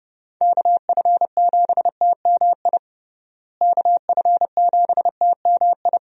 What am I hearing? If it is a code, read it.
Morse code, heard as a keyed tone, about 32 wpm, 700 Hz. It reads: KF7TMI KF7TMI
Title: USA Call Signs - 6 Characters - Repeated 3x - 30wpm